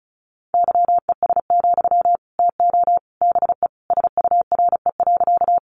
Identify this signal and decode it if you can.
Morse code, heard as a keyed tone, about 34 wpm, 700 Hz. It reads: YES, TO BE SURE.
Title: Sentences from Top 500 Words 35wpm